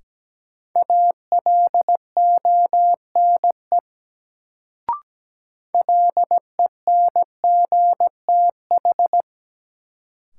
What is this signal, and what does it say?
Morse code, heard as a keyed tone, about 17 wpm, 700 Hz. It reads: ALONE LENGTH
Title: Sentences from Top 1000 Words - New Word Drill - 17wpm